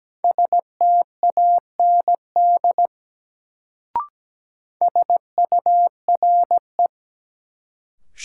Morse code, heard as a keyed tone, about 17 wpm, 700 Hz. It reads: STAND SURE
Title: Sentences from Top 300 Words - New Word Drill - 17wpm